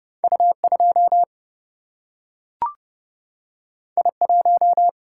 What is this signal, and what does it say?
Morse code, heard as a keyed tone, about 32 wpm, 700 Hz. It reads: U 2 I 1
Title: Single-Letter Single-Number 30wpm